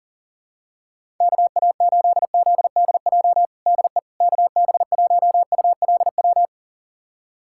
Morse code, heard as a keyed tone, about 37 wpm, 700 Hz. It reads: KA8ZDJ DE KB1ULW
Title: USA Callsign DE Callsign 40wpm